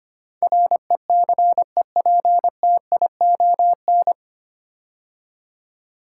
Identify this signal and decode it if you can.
Morse code, heard as a keyed tone, about 25 wpm, 700 Hz. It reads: RECEPTION
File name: practice / 25wpm / 9 Letter Words - First 3 Letters Given - 25wpm